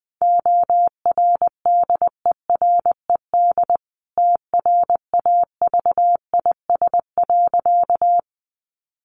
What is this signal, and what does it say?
Morse code, heard as a keyed tone, about 20 wpm, 700 Hz. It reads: ORDERED TRAVIS.
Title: A Sound Of Thunder 20wpm